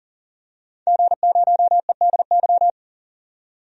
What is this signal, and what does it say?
Morse code, heard as a keyed tone, about 38 wpm, 700 Hz. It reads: G0EDY
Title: DXCC-100 Call Signs - 5 Characters - 40wpm